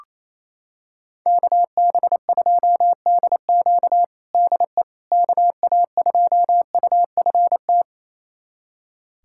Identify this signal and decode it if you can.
Morse code, heard as a keyed tone, about 27 wpm, 700 Hz. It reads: KB2DQ DE KA2UFT
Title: USA Callsign DE Callsign 28wpm